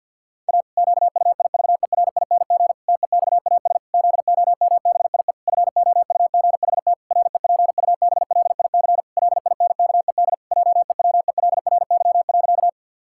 Morse code, heard as a keyed tone, about 51 wpm, 700 Hz. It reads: A QUIVERING TEXAS ZOMBIE FOUGHT REPUBLIC LINKED JEWELRY.
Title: Pangram Challenge v2 - 50wpm